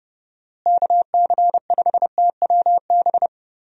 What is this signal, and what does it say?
Morse code, heard as a keyed tone, about 30 wpm, 700 Hz. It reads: KC5TWB
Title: USA Call Signs - 6 Characters - Repeated 2x - 30wpm